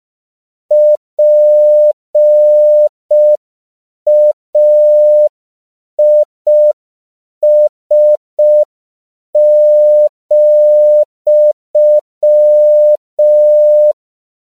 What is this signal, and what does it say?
Morse code, heard as a keyed tone, about 5 wpm, 600 Hz. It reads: PAIS,